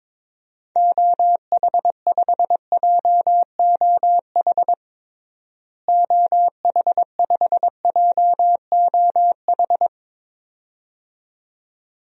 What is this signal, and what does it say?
Morse code, heard as a keyed tone, about 22 wpm, 700 Hz. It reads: OH5JOH OH5JOH
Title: DXCC-100 Call Signs - 6 Characters - Repeated 2x - 22wpm